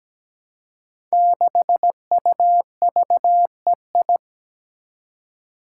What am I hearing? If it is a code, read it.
Morse code, heard as a keyed tone, about 17 wpm, 700 Hz. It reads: 6UVEI
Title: Five Challenging Letter-Number Combinations 17wpm